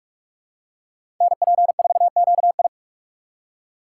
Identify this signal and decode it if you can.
Morse code, heard as a keyed tone, about 42 wpm, 700 Hz. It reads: NP4QI
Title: USA Call Signs - 2x Character Spacing - 45wpm